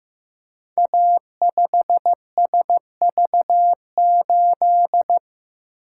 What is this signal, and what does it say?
Morse code, heard as a keyed tone, about 15 wpm, 700 Hz. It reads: A5SV8
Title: Code Groups - 5 Letter:Numbers - 15wpm